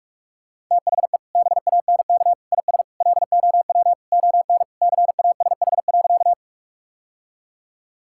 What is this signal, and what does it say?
Morse code, heard as a keyed tone, about 46 wpm, 700 Hz. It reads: THE BANK IS LOW ON CASH.
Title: Sentences from Top 1000 Words - Review Part 5 - 45wpm